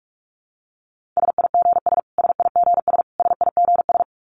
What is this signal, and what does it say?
Morse code, heard as a keyed tone, about 44 wpm, 700 Hz. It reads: SIGS SIGS SIGS
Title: Abbreviations Repeated 3x - 45wpm